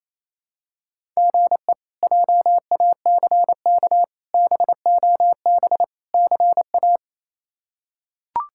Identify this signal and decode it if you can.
Morse code, heard as a keyed tone, about 28 wpm, 700 Hz. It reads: GE JACK BOB CA
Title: K1USN SST QSO Practice - 28wpm